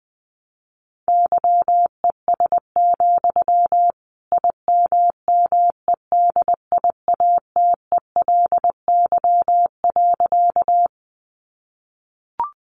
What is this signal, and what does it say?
Morse code, heard as a keyed tone, about 20 wpm, 700 Hz. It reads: YES, IMMEDIATELY.
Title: Sentences from Top 1100 Words 20wpm